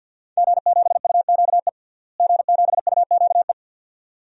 Morse code, heard as a keyed tone, about 49 wpm, 700 Hz. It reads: G7UQE G7UQE
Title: DXCC-100 Call Signs - 5 Characters - Repeated 2x - 50wpm